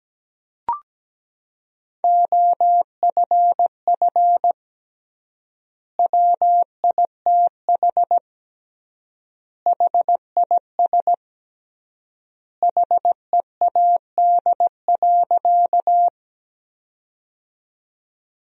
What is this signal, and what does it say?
Morse code, heard as a keyed tone, about 17 wpm, 700 Hz. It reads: OFF WITH HIS HEAD.
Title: 3x Word-Spacing - Sentences from Top 300 Words - 17wpm